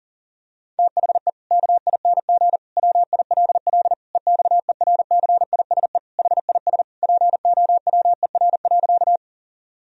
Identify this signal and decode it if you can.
Morse code, heard as a keyed tone, about 40 wpm, 700 Hz. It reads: THE KING WILL EXERCISE HIS POWER.
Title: Sentences from Top 1000 Words - Review Part 4 - 40wpm